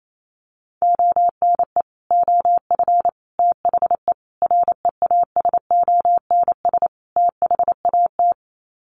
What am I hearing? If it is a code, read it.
Morse code, heard as a keyed tone, about 28 wpm, 700 Hz. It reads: ONE OF THE REASONS THAT
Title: Sets of 5 Words Encore 28wpm